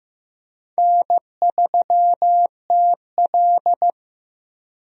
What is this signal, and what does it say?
Morse code, heard as a keyed tone, about 15 wpm, 700 Hz. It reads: N3TL
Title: USA Call Signs - 4 Characters - Repeated 2x - 15wpm